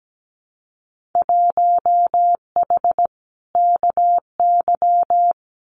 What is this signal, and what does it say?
Morse code, heard as a keyed tone, about 17 wpm, 700 Hz. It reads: 1H KY
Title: Winter Field Day Exchanges 17wpm